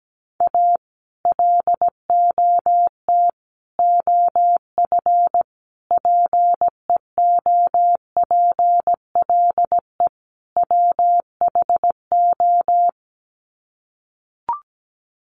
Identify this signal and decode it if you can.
Morse code, heard as a keyed tone, about 17 wpm, 700 Hz. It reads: A LOT OF PEOPLE WHO
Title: Sets of 5 Words 17wpm